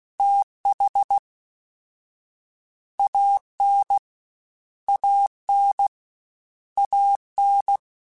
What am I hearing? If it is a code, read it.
Morse code, heard as a keyed tone, about 16 wpm, 800 Hz. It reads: TH AN AN AN
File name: Short Runs / Short Runs-mp3s / ShortContrasts @15